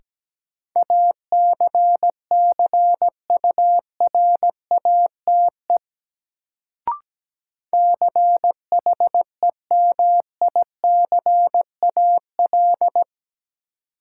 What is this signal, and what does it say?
Morse code, heard as a keyed tone, about 17 wpm, 700 Hz. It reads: ACCURATE CHEMICAL
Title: Sentences from Top 1200 Words - New Word Drill Alt - 17wpm